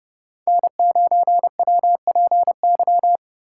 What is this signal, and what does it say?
Morse code, heard as a keyed tone, about 28 wpm, 700 Hz. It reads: N9WPY